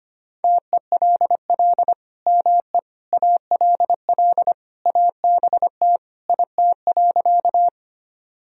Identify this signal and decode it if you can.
Morse code, heard as a keyed tone, about 25 wpm, 700 Hz. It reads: TELL ME ALL ABT IT.
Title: Sentences from Top 200 Words and Abbreviations 25wpm